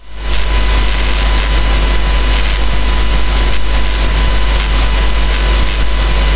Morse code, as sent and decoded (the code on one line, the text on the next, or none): none